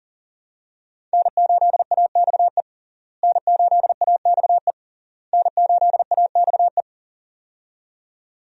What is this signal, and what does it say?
Morse code, heard as a keyed tone, about 40 wpm, 700 Hz. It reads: N8AXE N8AXE N8AXE
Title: USA Call Signs - 5 Characters - Repeated 3x - 40wpm